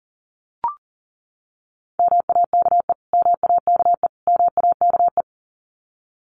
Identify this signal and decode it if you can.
Morse code, heard as a keyed tone, about 40 wpm, 700 Hz. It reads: MAKE MAKE MAKE
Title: Top 200 Words Repeated 40wpm